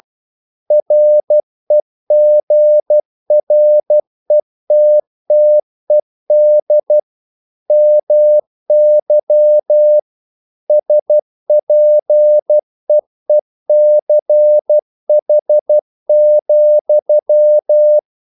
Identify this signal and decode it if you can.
Morse code, heard as a keyed tone, about 12 wpm, 600 Hz. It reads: REGRETTED MY SPEECH,
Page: English quotations in CW (telegraphy - morse)